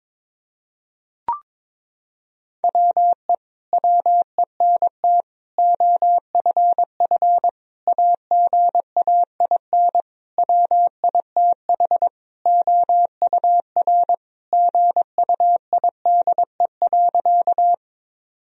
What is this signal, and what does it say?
Morse code, heard as a keyed tone, about 22 wpm, 700 Hz. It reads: WE WENT OFF AGAIN WITH OUR GUIDE.
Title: Sentences from Top 1000 Words - Review Part 3 - 22wpm